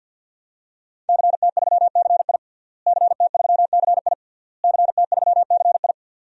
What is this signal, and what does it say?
Morse code, heard as a keyed tone, about 49 wpm, 700 Hz. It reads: CT3CI CT3CI CT3CI
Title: DXCC-100 Call Signs - 5 Characters - Repeated 3x - 50wpm